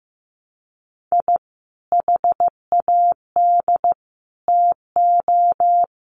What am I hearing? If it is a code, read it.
Morse code, heard as a keyed tone, about 15 wpm, 700 Hz. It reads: I HAD TO
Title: Sets of 3 Words 15wpm